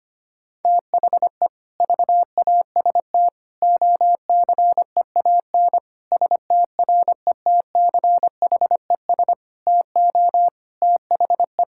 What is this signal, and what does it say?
Morse code, heard as a keyed tone, about 25 wpm, 700 Hz. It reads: THE VAST OCEAN STRETCHES TO THE
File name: practice / 25wpm / Difficult Sentences with Hints - 25wpm